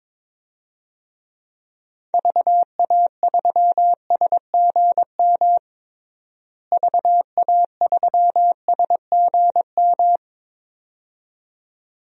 Morse code, heard as a keyed tone, about 21 wpm, 700 Hz. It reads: VA3SGM VA3SGM
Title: DXCC-100 Call Signs - 6 Characters - Repeated 2x - 22wpm